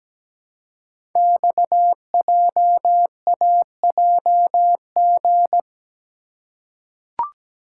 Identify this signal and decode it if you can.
Morse code, heard as a keyed tone, about 17 wpm, 700 Hz. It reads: XJAJG
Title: Code Groups - 5 Letters - 17wpm